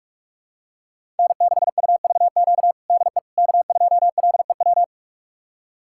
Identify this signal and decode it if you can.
Morse code, heard as a keyed tone, about 41 wpm, 700 Hz. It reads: N6UVQ DE K2LEW
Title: USA Callsign DE Callsign - 5 Characters - 45wpm